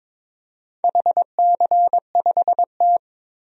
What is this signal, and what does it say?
Morse code, heard as a keyed tone, about 22 wpm, 700 Hz. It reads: HC5T
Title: DXCC-100 Call Signs - 4 Characters - 22wpm